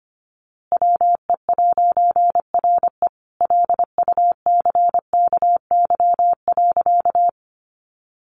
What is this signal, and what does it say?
Morse code, heard as a keyed tone, about 25 wpm, 700 Hz. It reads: WE'RE LUCKY.
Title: A Sound Of Thunder 25wpm